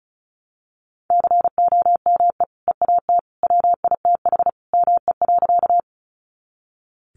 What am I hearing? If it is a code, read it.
Morse code, heard as a keyed tone, about 34 wpm, 700 Hz. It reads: COME EAT WITH ME.
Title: Sentences from Top 300 Words 35wpm